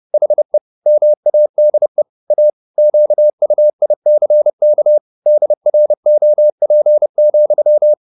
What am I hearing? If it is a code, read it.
Morse code, heard as a keyed tone, about 30 wpm, 600 Hz. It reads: HE MADE A QUICK DROP,
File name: the old man and the sea / the old man and the sea 1 30wpm